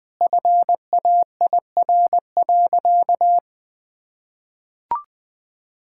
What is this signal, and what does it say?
Morse code, heard as a keyed tone, about 21 wpm, 700 Hz. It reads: FAIR.
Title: Sentences with 8-Word Phrases - 20wpm